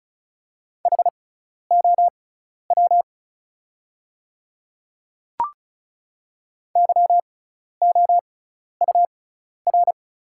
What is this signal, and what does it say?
Morse code, heard as a keyed tone, about 36 wpm, spaced as for 12 wpm, 700 Hz. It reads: HOW YOUR
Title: 6x Character Spacing - Top 100 Words - 35wpm